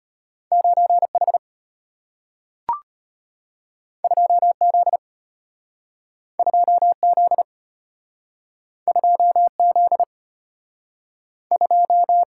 Morse code, heard as a keyed tone, about 34 wpm, 700 Hz. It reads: 9H 2Z 2Z 2Z 2